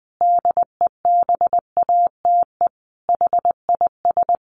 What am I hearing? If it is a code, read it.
Morse code, heard as a keyed tone, about 20 wpm, 700 Hz. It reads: DEBATE HIS
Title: Sentences from Top 1100 Words 20wpm